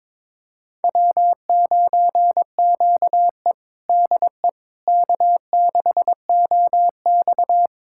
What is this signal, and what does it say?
Morse code, heard as a keyed tone, about 21 wpm, 700 Hz. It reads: W9QE DE K6OX
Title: USA Callsign DE Callsign - 4 Characters - 22wpm